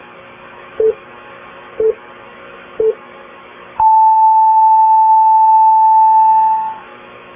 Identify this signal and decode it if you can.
Radio time signal (time pips). It is heard from 0.8 to 6.8 s.